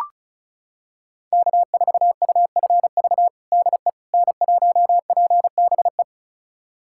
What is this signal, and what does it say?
Morse code, heard as a keyed tone, about 36 wpm, 700 Hz. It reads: K4UFV DE N1PBE
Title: USA Callsign DE Callsign - 5 Characters - 35wpm